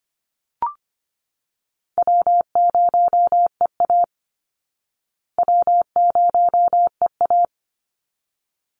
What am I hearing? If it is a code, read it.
Morse code, heard as a keyed tone, about 25 wpm, 700 Hz. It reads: W0EA W0EA
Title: Call Signs Repeated 3x - 25wpm